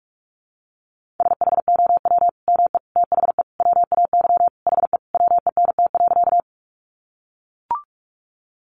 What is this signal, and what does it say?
Morse code, heard as a keyed tone, about 44 wpm, 700 Hz. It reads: SHOW ME THE WAY HE WENT.
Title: Sentences from Top 200 Words 45wpm